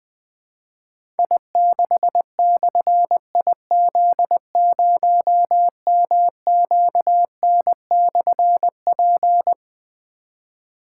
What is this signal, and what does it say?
Morse code, heard as a keyed tone, about 20 wpm, 700 Hz. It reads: I6/IZ0MQN/P
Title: Difficult Call Signs 20wpm